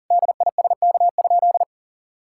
Morse code, heard as a keyed tone, about 40 wpm, 700 Hz. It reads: DISK?
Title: Sentences from Top 1000 Words - Review Part 6 - 40wpm